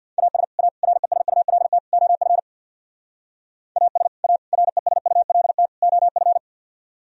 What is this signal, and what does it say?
Morse code, heard as a keyed tone, about 60 wpm, 700 Hz. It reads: AS A RESULT OF AS A RESULT OF
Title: Speed-Racing - Sets of 4 Words 50wpm